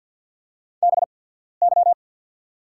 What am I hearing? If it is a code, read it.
Morse code, heard as a keyed tone, about 50 wpm, spaced as for 14 wpm, 700 Hz. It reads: BY
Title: 8x Character Spacing - Top 100 Words - 50wpm